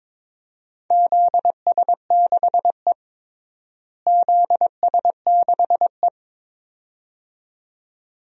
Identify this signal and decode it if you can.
Morse code, heard as a keyed tone, about 22 wpm, 700 Hz. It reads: ZS6E ZS6E
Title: DXCC-100 Call Signs - 4 Characters - Repeated 2x - 22wpm